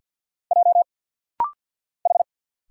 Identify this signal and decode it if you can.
Morse code, heard as a keyed tone, about 50 wpm, 700 Hz. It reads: J H
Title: Single Letter-Number Alt - Rapid-Fire 50wpm